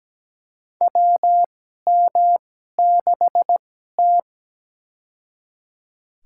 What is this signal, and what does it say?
Morse code, heard as a keyed tone, about 17 wpm, spaced as for 12 wpm, 700 Hz. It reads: WM6T
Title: USA Call Signs - 4 Characters - 2x Character Spacing - 17wpm